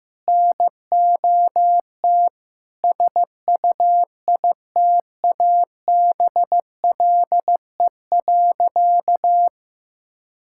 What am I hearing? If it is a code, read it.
Morse code, heard as a keyed tone, about 15 wpm, 700 Hz. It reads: NOT SUITABLE.
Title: Sentences from Top 1600 Words 15wpm